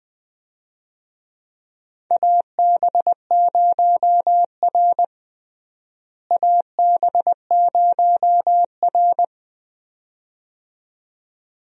Morse code, heard as a keyed tone, about 20 wpm, 700 Hz. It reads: AB0R AB0R
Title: USA Call Signs - 4 Characters - Repeated 2x - 20wpm